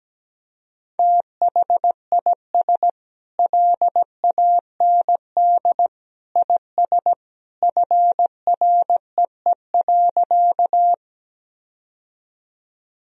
Morse code, heard as a keyed tone, about 17 wpm, 700 Hz. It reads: THIS LAND IS FREE.
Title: Sentences from Top 1000 Words - Review Part 1 - 17wpm